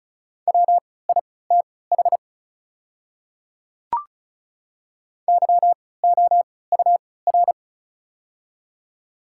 Morse code, heard as a keyed tone, about 36 wpm, spaced as for 20 wpm, 690 Hz. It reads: WITH YOUR
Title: 3x Character Spacing - Top 100 Words - 35wpm